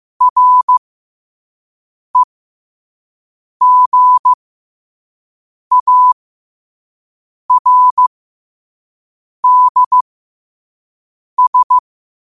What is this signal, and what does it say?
Morse code, heard as a keyed tone, about 15 wpm, spaced as for 5 wpm, 1000 Hz. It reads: REGARDS
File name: qotdfiles / 5wpm